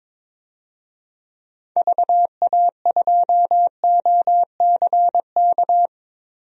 Morse code, heard as a keyed tone, about 22 wpm, 700 Hz. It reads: VA2OCK
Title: DXCC-100 Call Signs - 6 Characters - 22wpm